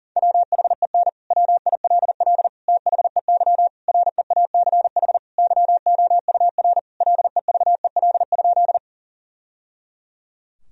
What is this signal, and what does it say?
Morse code, heard as a keyed tone, about 40 wpm, 700 Hz. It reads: WHEN WILL THEY REACH YOUR LEVEL?
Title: Sentences from Top 1000 Words - Review Part 4 - 40wpm